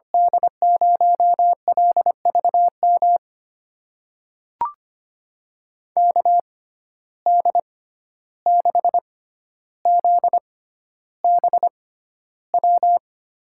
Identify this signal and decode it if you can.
Morse code, heard as a keyed tone, about 25 wpm, spaced as for 12 wpm, 700 Hz. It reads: D0LVM KD6ZBW
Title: USA Call Signs - 6 Characters - 6x Character Spacing - 25wpm